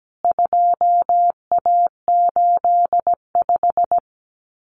Morse code, heard as a keyed tone, about 17 wpm, 700 Hz. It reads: 2A85
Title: CWT Exchange 17wpm